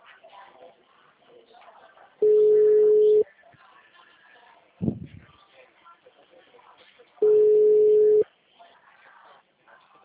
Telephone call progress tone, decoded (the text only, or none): ringback tone